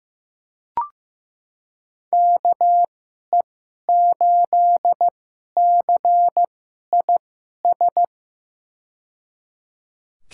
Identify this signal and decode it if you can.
Morse code, heard as a keyed tone, about 15 wpm, spaced as for 11 wpm, 700 Hz. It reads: KE8CIS
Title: USA Call Signs - 6 Characters - 2x Character Spacing - 15wpm